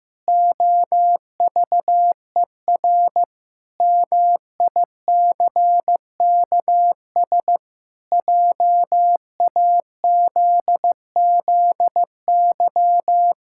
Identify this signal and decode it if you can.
Morse code, heard as a keyed tone, about 16 wpm, 700 Hz. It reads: OVER MICKS JAZZY
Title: Pangram Challenge v2 - 15wpm